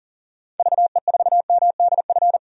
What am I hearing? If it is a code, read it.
Morse code, heard as a keyed tone, about 40 wpm, 700 Hz. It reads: VE4MDF
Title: DXCC-100 Call Signs - 6 Characters - 40wpm